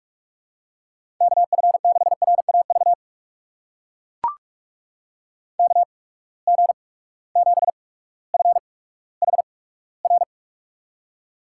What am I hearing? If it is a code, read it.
Morse code, heard as a keyed tone, about 44 wpm, spaced as for 19 wpm, 700 Hz. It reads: KF6RAV KG7FHR
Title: USA Call Signs - 6 Characters - 8x Character Spacing - 45wpm